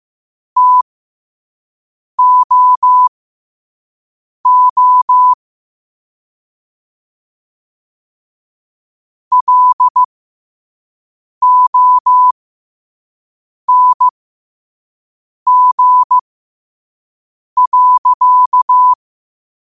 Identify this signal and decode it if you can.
Morse code, heard as a keyed tone, about 15 wpm, spaced as for 5 wpm, 1000 Hz. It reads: TOO LONG.
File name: qotdfiles / 5wpm